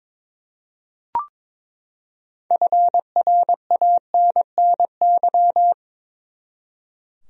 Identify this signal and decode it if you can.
Morse code, heard as a keyed tone, about 22 wpm, 700 Hz. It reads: FRANNY
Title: USA Common Names - 22wpm